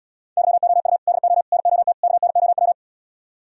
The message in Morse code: YOU DO NOT KNOW

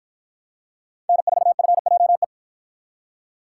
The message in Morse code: N4FJE

